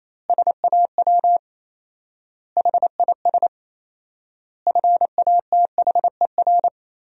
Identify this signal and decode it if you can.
Morse code, heard as a keyed tone, about 28 wpm, 700 Hz. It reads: SAW HIS FATHER
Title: 4x Word-Spacing - Sentences from Top 500 Words - 28wpm